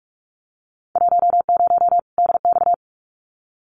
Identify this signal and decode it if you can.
Morse code, heard as a keyed tone, about 46 wpm, 700 Hz. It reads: 10 DX
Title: Winter Field Day Exchanges 45wpm